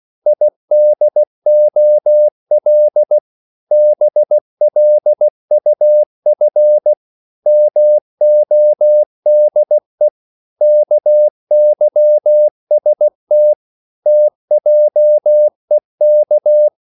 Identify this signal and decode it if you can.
Morse code, heard as a keyed tone, about 16 wpm, 600 Hz. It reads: IDOL BLUF MODE KYST TJEK
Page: Korte danske ord 16wpm